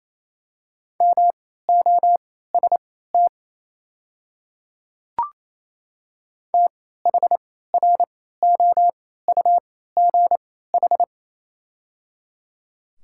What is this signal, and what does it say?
Morse code, heard as a keyed tone, about 28 wpm, spaced as for 16 wpm, 700 Hz. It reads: MOST THROUGH
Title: 3x Character Spacing - Top 200 Words - 28wpm